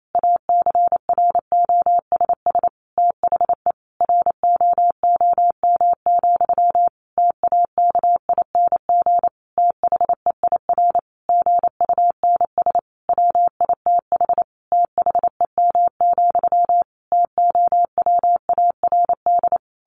Morse code, heard as a keyed tone, about 28 wpm, 700 Hz. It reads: ACROSS THE ROOM, TAKING THEIR GUNS WITH THEM, TOWARD
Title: A Sound Of Thunder 28wpm